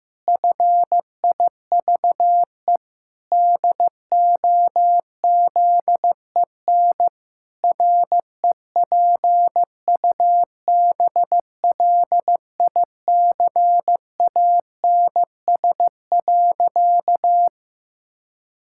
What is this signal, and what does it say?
Morse code, heard as a keyed tone, about 15 wpm, 700 Hz. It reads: FIVE DOZEN REPUBLICANS.
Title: Pangram Challenge v2 - 15wpm